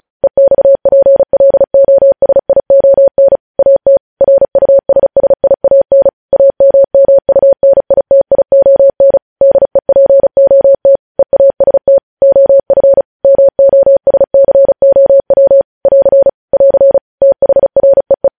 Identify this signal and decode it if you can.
Morse code, heard as a keyed tone, about 36 wpm, 550 Hz. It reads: EXPLOSION AT RUSSIAN AMMUNITION DEPOT EAST OF MOSCOW <AR> <AR> THREE
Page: news headlines in morse code at 35 wpm